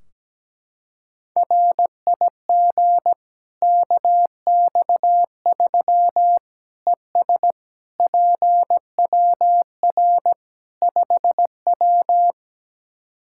Morse code, heard as a keyed tone, about 17 wpm, 700 Hz. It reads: RIG KX3 ES PWR 5W
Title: Practice QSO - Exchange 2 - Info - 17wpm